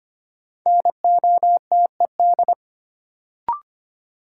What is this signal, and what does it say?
Morse code, heard as a keyed tone, about 25 wpm, 700 Hz. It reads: NOTED
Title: Sets of 5 Words - New Word Drill - 25wpm